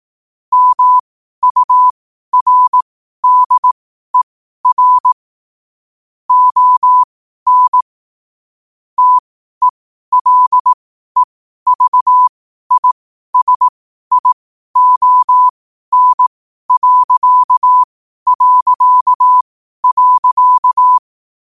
Morse code, heard as a keyed tone, about 18 wpm, spaced as for 12 wpm, 1000 Hz. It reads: MURDER ON TELEVISION...
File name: qotdfiles / 12wpm